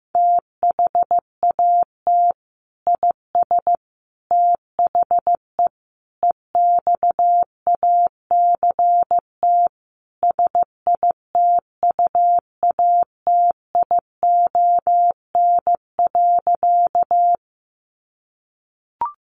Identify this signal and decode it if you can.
Morse code, heard as a keyed tone, about 15 wpm, 700 Hz. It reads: THAT IS THE EXACT SITUATION.
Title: Sentences from Top 1300 Words 15wpm